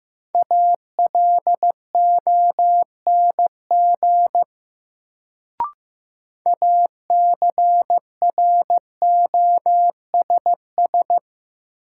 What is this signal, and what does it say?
Morse code, heard as a keyed tone, about 15 wpm, 700 Hz. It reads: ALONG ACROSS
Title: Sentences from Top 700 Words - New Word Drill - 15wpm